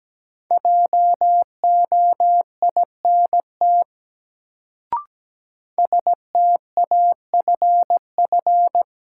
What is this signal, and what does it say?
Morse code, heard as a keyed tone, about 17 wpm, 700 Hz. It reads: JOINT STAFF
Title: Sets of 5 Words Encore - New Word Drill - 17wpm